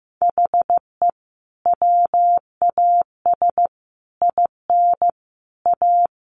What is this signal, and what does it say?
Morse code, heard as a keyed tone, about 14 wpm, 700 Hz. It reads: HE WAS IN A
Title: Sentences from Top 1200 Words 15wpm